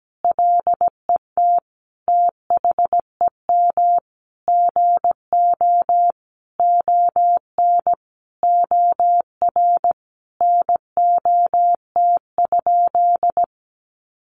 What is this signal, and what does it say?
Morse code, heard as a keyed tone, about 17 wpm, 700 Hz. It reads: LET THEM GO ON OR NOT?
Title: Sentences from Top 600 Words 17wpm